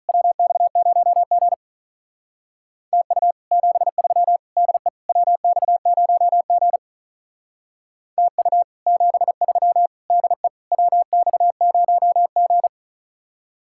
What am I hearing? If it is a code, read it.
Morse code, heard as a keyed tone, about 43 wpm, 700 Hz. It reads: WX0G TU 73 DE WX0G TU 73 DE WX0G
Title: Speed-Racing Common QSO Phrases Encore - 35wpm